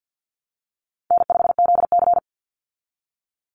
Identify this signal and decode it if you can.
Morse code, heard as a keyed tone, about 49 wpm, 700 Hz. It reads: N5ZC
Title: Easy Call Signs Repeated 2x - 50wpm